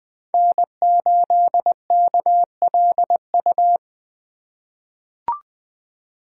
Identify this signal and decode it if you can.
Morse code, heard as a keyed tone, about 20 wpm, 700 Hz. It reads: N8KLU
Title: USA Call Signs - 5 Characters - 4x Character Spacing - 20wpm